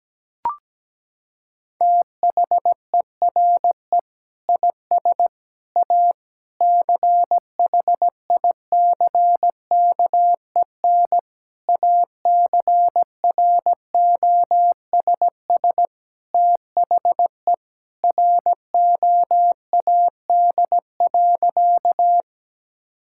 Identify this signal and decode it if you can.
Morse code, heard as a keyed tone, about 17 wpm, 700 Hz. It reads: THERE IS A CHICKEN ACROSS THE ROAD.
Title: Sentences from Top 1000 Words - Review Part 5 - 17wpm